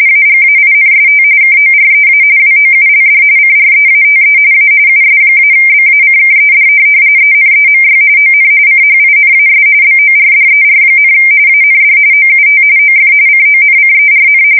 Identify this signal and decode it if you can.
RTTY, about 45.45 baud. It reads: AGAINST KOREA WOULD CREATE A US-BACKED ENEMY ON CHINAS BORDER, AND OUT OF CONCERN
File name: fsk170 / USS-Missouri